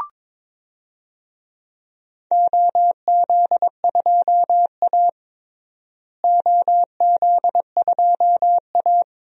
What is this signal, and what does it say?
Morse code, heard as a keyed tone, about 22 wpm, 700 Hz. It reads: OZ2A OZ2A
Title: DXCC-100 Call Signs - 4 Characters - Repeated 3x - 22wpm